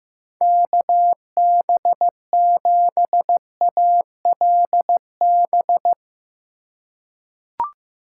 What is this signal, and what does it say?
Morse code, heard as a keyed tone, about 15 wpm, 700 Hz. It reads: KB7ALB
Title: USA Call Signs - 6 Characters - 4x Character Spacing - 15wpm